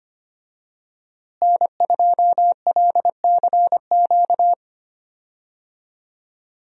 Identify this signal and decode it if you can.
Morse code, heard as a keyed tone, about 25 wpm, 700 Hz. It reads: N2LCQ
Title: USA Call Signs - 5 Characters - Repeated 2x - 25wpm